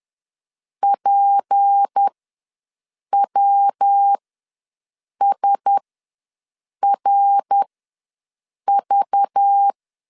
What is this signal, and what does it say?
Morse code, heard as a keyed tone, about 10 wpm, spaced as for 6 wpm, 800 Hz. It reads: PWSRV
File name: LEZIONI / 6 PKX0 / 6 PK ws 10wpm 3745